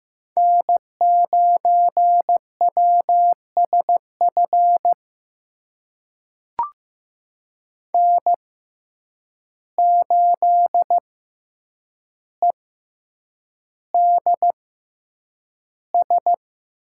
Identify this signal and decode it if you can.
Morse code, heard as a keyed tone, about 15 wpm, spaced as for 8 wpm, 700 Hz. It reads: N9WSF N8EDS
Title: USA Call Signs - 5 Characters - 6x Character Spacing - 15wpm